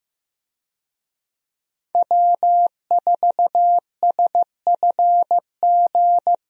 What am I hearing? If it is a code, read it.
Morse code, heard as a keyed tone, about 15 wpm, 700 Hz. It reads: W4SFG
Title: USA Call Signs - 5 Characters - Repeated 2x - 15wpm